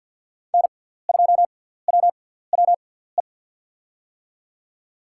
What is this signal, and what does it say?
Morse code, heard as a keyed tone, about 51 wpm, spaced as for 17 wpm, 700 Hz. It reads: N2WWE
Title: USA Call Signs - 5 Characters - 6x Character Spacing - 50wpm